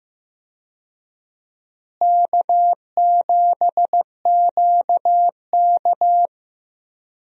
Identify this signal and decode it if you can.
Morse code, heard as a keyed tone, about 15 wpm, 700 Hz. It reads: K7QK
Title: USA Call Signs - 4 Characters - Repeated 2x - 15wpm